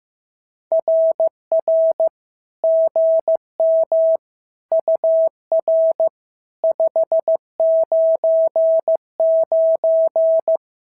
Morse code, heard as a keyed tone, about 15 wpm, 650 Hz. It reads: RR GM UR 599
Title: SOTA QSO Practice 15wpm